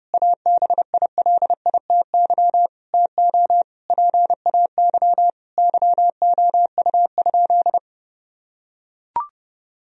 Morse code, heard as a keyed tone, about 30 wpm, 700 Hz. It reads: ABILITY TO PAY YOU?
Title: Sentences from Top 1000 Words - Review Part 3 - 30wpm